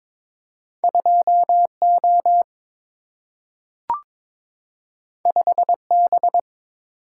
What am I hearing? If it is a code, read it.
Morse code, heard as a keyed tone, about 22 wpm, 700 Hz. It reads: 2 O 5 B
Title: Single-Letter Single-Number 22wpm